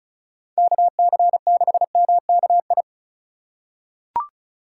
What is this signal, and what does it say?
Morse code, heard as a keyed tone, about 34 wpm, 700 Hz. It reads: KC6MKI
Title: USA Call Signs - 6 Characters - 3x Character Spacing - 35wpm